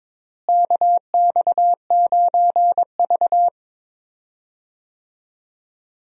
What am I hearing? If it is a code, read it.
Morse code, heard as a keyed tone, about 22 wpm, 700 Hz. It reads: KX9V
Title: USA Call Signs - 4 Characters - Repeated 2x - 22wpm